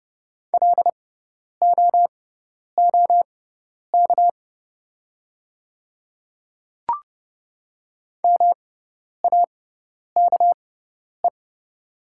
Morse code, heard as a keyed tone, about 30 wpm, spaced as for 10 wpm, 700 Hz. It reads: LOOK MAKE
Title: 6x Character Spacing - Top 100 Words - 30wpm